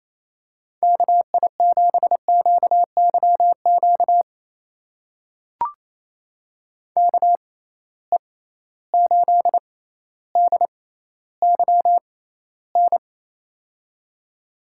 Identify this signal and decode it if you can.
Morse code, heard as a keyed tone, about 28 wpm, spaced as for 14 wpm, 700 Hz. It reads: KI7QYQ KE8DYN
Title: USA Call Signs - 6 Characters - 6x Character Spacing - 28wpm